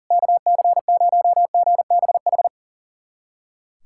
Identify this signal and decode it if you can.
Morse code, heard as a keyed tone, about 40 wpm, 700 Hz. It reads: KC0GBH
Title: USA Callsign DE Callsign 40wpm